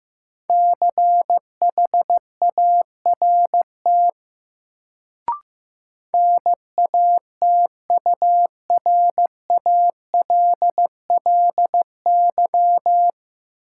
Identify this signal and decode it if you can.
Morse code, heard as a keyed tone, about 15 wpm, 700 Hz. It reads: CHART NATURALLY
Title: Sentences from Top 1300 Words - New Word Drill Alt - 15wpm